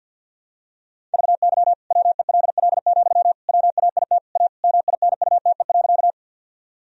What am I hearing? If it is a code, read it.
Morse code, heard as a keyed tone, about 52 wpm, 700 Hz. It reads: VY WELL, WAIT A MINUTE.
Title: Sentences from Top 500 Words and Abbreviations 50wpm